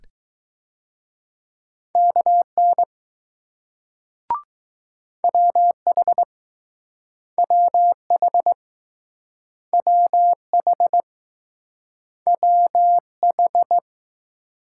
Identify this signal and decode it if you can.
Morse code, heard as a keyed tone, about 20 wpm, 700 Hz. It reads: KN WH WH WH WH